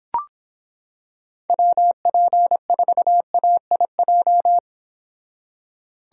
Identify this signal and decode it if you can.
Morse code, heard as a keyed tone, about 26 wpm, 700 Hz. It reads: WP4AIJ